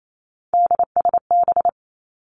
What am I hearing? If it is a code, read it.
Morse code, heard as a keyed tone, about 28 wpm, 700 Hz. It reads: DSB
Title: Three Challenging Letter-Number Combinations 28wpm